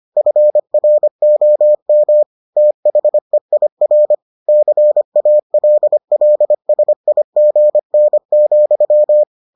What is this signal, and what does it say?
Morse code, heard as a keyed tone, about 25 wpm, 600 Hz. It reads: FROM THEIR CALLSIGN,